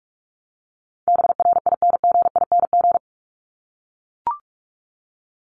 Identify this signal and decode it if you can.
Morse code, heard as a keyed tone, about 44 wpm, 700 Hz. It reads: BRINGING
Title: ING Suffix - Top 100 Words with Base Words 45wpm